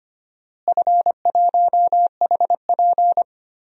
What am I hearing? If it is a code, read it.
Morse code, heard as a keyed tone, about 25 wpm, 700 Hz. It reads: F1HP
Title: DXCC-100 Call Signs - 4 Characters - Repeated 3x - 25wpm